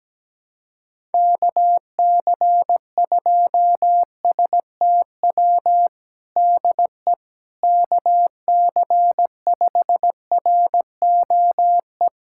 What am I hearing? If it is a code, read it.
Morse code, heard as a keyed tone, about 17 wpm, 700 Hz. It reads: KC2STW DE KC5ROE
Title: USA Callsign DE Callsign - 6 Characters - 17wpm